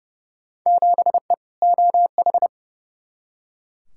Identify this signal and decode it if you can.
Morse code, heard as a keyed tone, about 30 wpm, 700 Hz. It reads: 7E OH
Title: Field Day Exchanges Encore - 30wpm